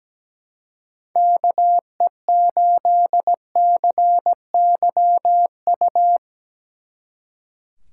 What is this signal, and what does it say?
Morse code, heard as a keyed tone, about 19 wpm, 700 Hz. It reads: KE8CYU
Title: Speed Racing - USA Call Signs 17wpm